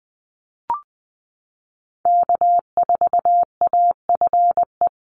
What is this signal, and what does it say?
Morse code, heard as a keyed tone, about 19 wpm, 700 Hz. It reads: K4AFE
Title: Call Signs Repeated 2x - 20wpm